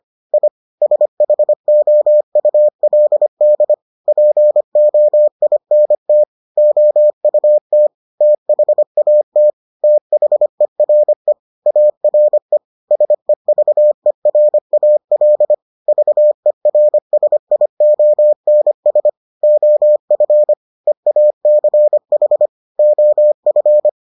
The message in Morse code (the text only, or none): I SHOULD POINT OUT THAT THERE ARE SEVERAL VERSIONS OF EACH OF